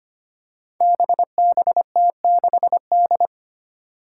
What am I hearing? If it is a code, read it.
Morse code, heard as a keyed tone, about 25 wpm, 700 Hz. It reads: BBT6D
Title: Five Challenging Letter-Number Combinations 25wpm